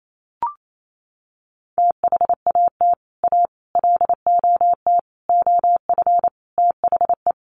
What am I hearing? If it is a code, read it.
Morse code, heard as a keyed tone, about 28 wpm, 700 Hz. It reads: THAT A LOT OF THE
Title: Sets of 5 Words Encore2 28wpm